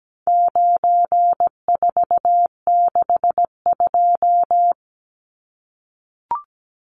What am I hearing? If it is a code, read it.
Morse code, heard as a keyed tone, about 17 wpm, 700 Hz. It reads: 9462
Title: Numbers - 4-Digit 17wpm